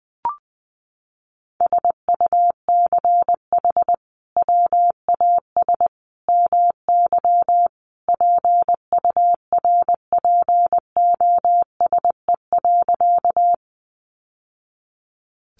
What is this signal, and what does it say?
Morse code, heard as a keyed tone, about 19 wpm, 700 Hz. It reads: SUCH WAS MY PURPOSE.
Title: Sentences from Top 800 Words 20wpm